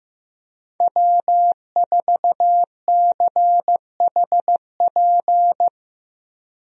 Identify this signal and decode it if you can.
Morse code, heard as a keyed tone, about 15 wpm, 700 Hz. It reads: W4CHP
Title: USA Call Signs - 5 Characters - Repeated 2x - 15wpm